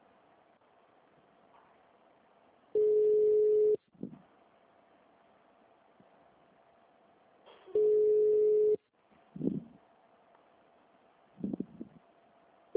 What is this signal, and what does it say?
Ringback tone (Europe, 425 Hz), with speech.
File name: out- -2016- - - .24